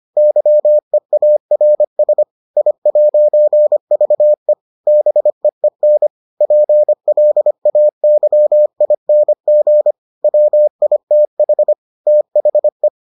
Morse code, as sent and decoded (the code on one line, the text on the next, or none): -.-- . .- .-. ... / .. .----. ...- . / -... . -. / .--. .-.. .- -.-- .. -. --. / .-- .. - .... / - .... .
YEARS I'VE BEEN PLAYING WITH THE